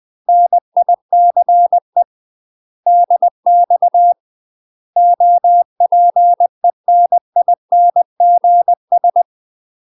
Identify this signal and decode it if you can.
Morse code, heard as a keyed tone, about 20 wpm, 700 Hz. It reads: NICE DX OPENINGS
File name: UK MASTER QSOs TRAINING / 20WPM / UK MASTER QSOs 0072